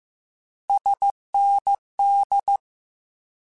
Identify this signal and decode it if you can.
Morse code, heard as a keyed tone, about 15 wpm, 800 Hz. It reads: SND
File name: Abreviations / Abreviations mp3s / Abreviations@13